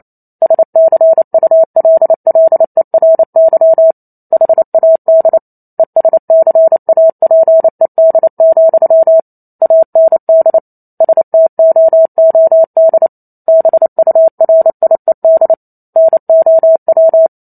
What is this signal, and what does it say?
Morse code, heard as a keyed tone, about 28 wpm, 650 Hz. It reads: SCULLERY HAD ESCAPED, AND STOOD BURIED NOW
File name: Chapter 2 - 28 WPM